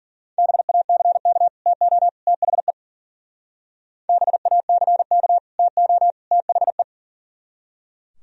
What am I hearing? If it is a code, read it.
Morse code, heard as a keyed tone, about 50 wpm, 700 Hz. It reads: BACK TO THE BACK TO THE
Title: Speed-Racing - Sets of 3 Words 40wpm